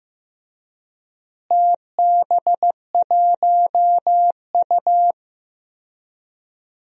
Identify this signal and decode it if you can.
Morse code, heard as a keyed tone, about 15 wpm, 700 Hz. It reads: TB1U
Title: DXCC-100 Call Signs - 4 Characters - Repeated 2x - 15wpm